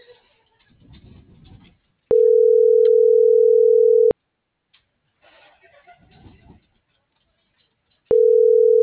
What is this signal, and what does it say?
Ringback tone (North America), 440 + 480 Hz.